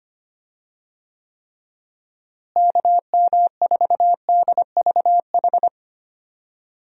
Morse code, heard as a keyed tone, about 25 wpm, 700 Hz. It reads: KM4DVH
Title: USA Call Signs - 6 Characters - Repeated 3x - 25wpm